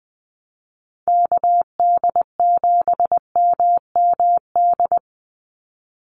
Morse code, heard as a keyed tone, about 20 wpm, 700 Hz. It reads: KD7MMD
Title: USA Call Signs 20wpm